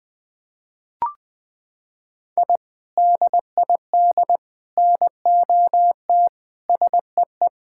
Morse code, heard as a keyed tone, about 20 wpm, 700 Hz. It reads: I DID NOT SEE
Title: Sentences with 3-Word Phrases - 20wpm